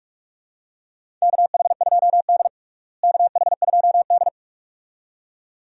Morse code, heard as a keyed tone, about 46 wpm, 700 Hz. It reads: KH2D KH2D
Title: USA Call Signs - 4 Characters - Repeated 2x - 45wpm